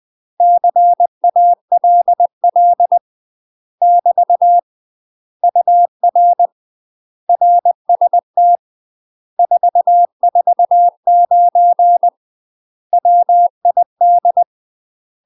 Morse code, heard as a keyed tone, about 20 wpm, 700 Hz. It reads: CALL <BT> UR RST 449 WID
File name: UK MASTER QSOs TRAINING / 20WPM / UK MASTER QSOs 0031